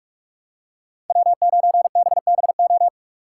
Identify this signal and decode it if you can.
Morse code, heard as a keyed tone, about 42 wpm, 700 Hz. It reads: W9BBO
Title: USA Call Signs - 5 Characters - 3x Character Spacing - 45wpm